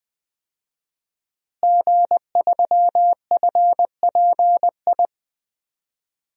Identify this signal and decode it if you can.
Morse code, heard as a keyed tone, about 21 wpm, 700 Hz. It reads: G3FPI
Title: DXCC-100 Call Signs - 5 Characters - Repeated 2x - 20wpm